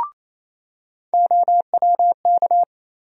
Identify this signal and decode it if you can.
Morse code, heard as a keyed tone, about 28 wpm, 700 Hz. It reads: OWK
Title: USA Call Sign Postfixes - 3 Letters 28wpm